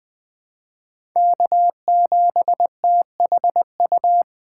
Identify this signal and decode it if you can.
Morse code, heard as a keyed tone, about 20 wpm, 700 Hz. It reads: K7THU
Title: USA Call Signs - 5 Characters - 3x Character Spacing - 20wpm